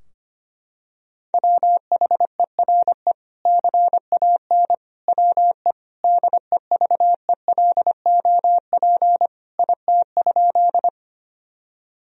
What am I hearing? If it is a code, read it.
Morse code, heard as a keyed tone, about 25 wpm, 700 Hz. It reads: WHERE CAN WE DEVELOP IT?
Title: Sentences from Top 1000 Words - Review Part 2 - 25wpm